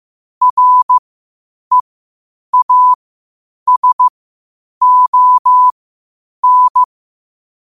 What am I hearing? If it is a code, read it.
Morse code, heard as a keyed tone, about 15 wpm, spaced as for 8 wpm, 1000 Hz. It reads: REASON